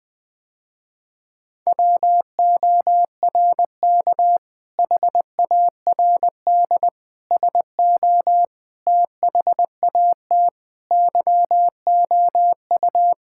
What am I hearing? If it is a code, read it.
Morse code, heard as a keyed tone, about 20 wpm, 700 Hz. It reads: WORK HARD SO THAT YOU
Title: Sentences with 2-Word Phrases - Spoken Beginning - 20wpm